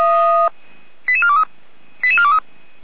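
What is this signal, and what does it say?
Five-tone selective call (ZVEI), heard as quick runs of five tones, tones of about 80 ms. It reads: 88423 88423
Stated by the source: ZVEI five-tone call 88387 received on an SDR.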